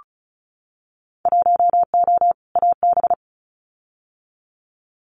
Morse code, heard as a keyed tone, about 36 wpm, 700 Hz. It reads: 1O AB
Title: Winter Field Day Exchanges 35wpm